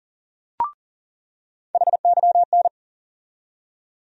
Morse code, heard as a keyed tone, about 40 wpm, 700 Hz. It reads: HYN
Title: USA Call Sign Postfixes - 3 Letters 40wpm